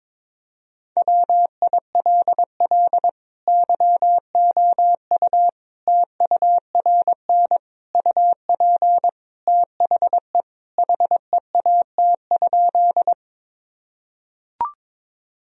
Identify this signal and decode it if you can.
Morse code, heard as a keyed tone, about 22 wpm, 700 Hz. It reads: WILL YOU TURN UP THE HEAT?
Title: Sentences from Top 1000 Words - Review Part 3 - 22wpm